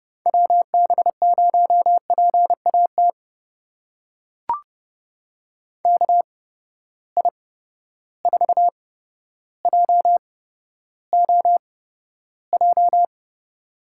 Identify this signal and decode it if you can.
Morse code, heard as a keyed tone, about 29 wpm, spaced as for 13 wpm, 700 Hz. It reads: WB0PAT KI4JOJ
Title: USA Call Signs - 6 Characters - 8x Character Spacing - 30wpm